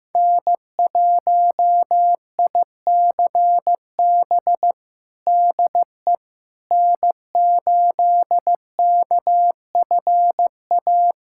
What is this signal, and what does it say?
Morse code, heard as a keyed tone, about 15 wpm, 700 Hz. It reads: N1ICB DE N8KFA
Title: USA Callsign DE Callsign - 5 Characters - 15wpm